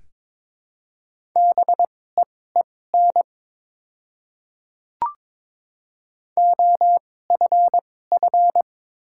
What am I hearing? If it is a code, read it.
Morse code, heard as a keyed tone, about 22 wpm, spaced as for 16 wpm, 700 Hz. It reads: BEEN OFF
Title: 2x Character Spacing - Top 200 Words - 22wpm